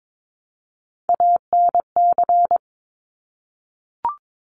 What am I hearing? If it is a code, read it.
Morse code, heard as a keyed tone, about 22 wpm, 700 Hz. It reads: ANC
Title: Top Three Letter Combinations in Words 22wpm